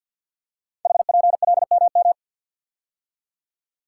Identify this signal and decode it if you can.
Morse code, heard as a keyed tone, about 50 wpm, 690 Hz. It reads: HPLMM
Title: Code Groups - 5 Letters - 50wpm